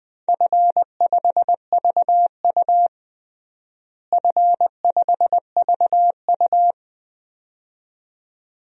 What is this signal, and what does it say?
Morse code, heard as a keyed tone, about 20 wpm, 700 Hz. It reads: F5VU F5VU
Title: DXCC-100 Call Signs - 4 Characters - Repeated 2x - 20wpm